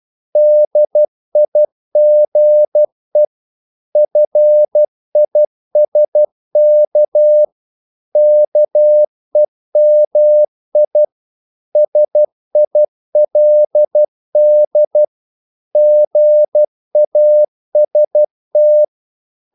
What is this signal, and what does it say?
Morse code, heard as a keyed tone, about 12 wpm, 600 Hz. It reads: DIGE FISK KEMI SILD GAST
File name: Korte ord DK 12wpm